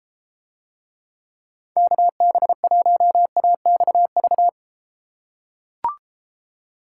Morse code, heard as a keyed tone, about 31 wpm, 700 Hz. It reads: KB1AXV